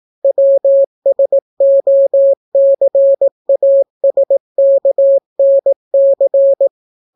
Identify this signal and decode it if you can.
Morse code, heard as a keyed tone, about 18 wpm, 550 Hz. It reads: WSOCASKNC